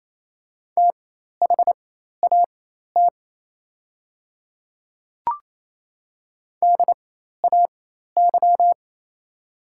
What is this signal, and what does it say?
Morse code, heard as a keyed tone, about 28 wpm, spaced as for 13 wpm, 700 Hz. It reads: THAT DAY
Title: 4x Character Spacing - Top 100 Words - 28wpm